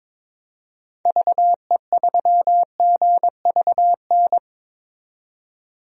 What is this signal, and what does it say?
Morse code, heard as a keyed tone, about 21 wpm, 700 Hz. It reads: VE3GVN
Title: DXCC-100 Call Signs - 6 Characters - 22wpm